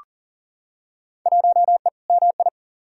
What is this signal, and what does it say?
Morse code, heard as a keyed tone, about 38 wpm, 700 Hz. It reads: 1E MI
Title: Field Day Exchanges - 40wpm